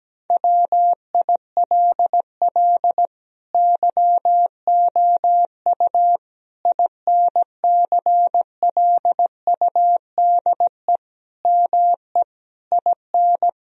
Morse code, heard as a keyed tone, about 17 wpm, 700 Hz. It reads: WILL YOU INCLUDE ME IN
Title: Sentences from Top 1000 Words - Review Part 3 - 17wpm